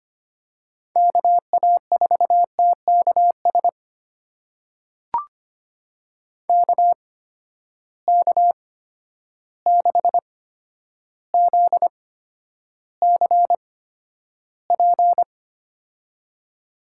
Morse code, heard as a keyed tone, about 25 wpm, spaced as for 11 wpm, 700 Hz. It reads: KA4TKS KK6ZCP
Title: USA Call Signs - 8x Character Spacing - 25wpm